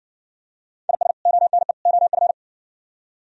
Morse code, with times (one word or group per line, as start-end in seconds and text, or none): 0.9-1.1 IS
1.3-1.7 ONE
1.9-2.3 OF